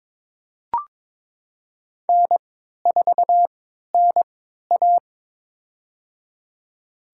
Morse code, heard as a keyed tone, about 21 wpm, spaced as for 12 wpm, 700 Hz. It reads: N4NA
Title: USA Call Signs - 4 Characters - 3x Character Spacing - 22wpm